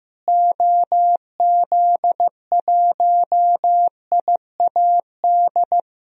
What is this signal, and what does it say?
Morse code, heard as a keyed tone, about 15 wpm, 700 Hz. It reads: OZ1IAD
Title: DXCC-100 Call Signs - 6 Characters - Repeated 2x - 15wpm